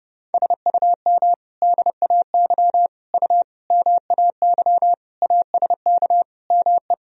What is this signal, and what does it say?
Morse code, heard as a keyed tone, about 32 wpm, 700 Hz. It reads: SUM DAY U MAY ASK ME
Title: Sentences from Top 200 Words and Abbreviations 30wpm